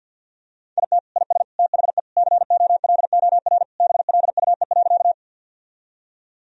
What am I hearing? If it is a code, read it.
Morse code, heard as a keyed tone, about 50 wpm, 700 Hz. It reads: IT IS THE COLOR BLUE.
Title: Sentences from Top 1000 Words - Review Part 3 - 50wpm